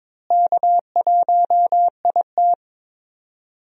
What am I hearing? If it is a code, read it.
Morse code, heard as a keyed tone, about 22 wpm, 700 Hz. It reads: K1IT
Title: USA Call Signs - 4 Characters - Repeated 2x - 22wpm